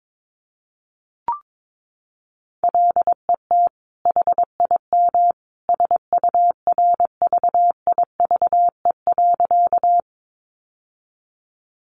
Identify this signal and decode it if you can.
Morse code, heard as a keyed tone, about 22 wpm, 700 Hz. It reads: LET HIM SURVIVE.
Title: Sentences from Top 1300 Words 22wpm